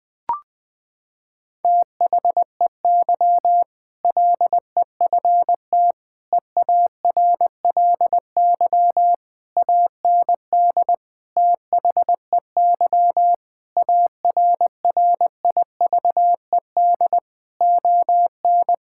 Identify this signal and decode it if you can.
Morse code, heard as a keyed tone, about 20 wpm, 700 Hz. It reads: THEY LEFT EARLY AND THEY ARRIVED ON
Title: Sentences with 2-Word Phrases - 20wpm